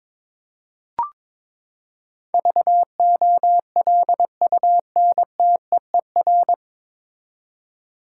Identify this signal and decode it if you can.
Morse code, heard as a keyed tone, about 21 wpm, 700 Hz. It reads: VOLUNTEER
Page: Morse Code Ninja practice 22wpm